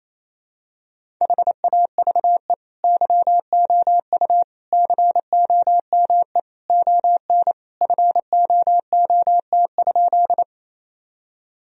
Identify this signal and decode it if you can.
Morse code, heard as a keyed tone, about 27 wpm, 700 Hz. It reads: HAVE YOU COME ON FOOT?
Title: Sentences from Top 1000 Words - Review Part 3 - 28wpm